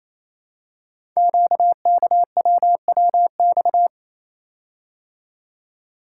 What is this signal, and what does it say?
Morse code, heard as a keyed tone, about 29 wpm, 700 Hz. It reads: QKWWX
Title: Code Groups - 5 Letters - 28wpm